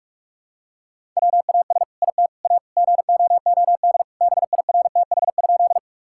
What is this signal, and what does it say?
Morse code, heard as a keyed tone, about 46 wpm, 700 Hz. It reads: WAS IT A GOOD BIRTH?
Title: Sentences from Top 1000 Words - Review Part 5 - 45wpm